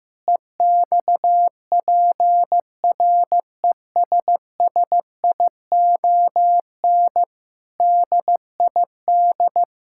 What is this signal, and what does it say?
Morse code, heard as a keyed tone, about 16 wpm, 700 Hz. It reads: EXPRESSION DID
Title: Sentences from Top 1600 Words 15wpm